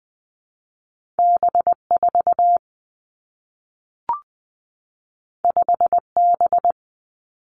Morse code, heard as a keyed tone, about 20 wpm, 700 Hz. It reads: B 4 5 B
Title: Two Challenging Letter-Number Combinations 20wpm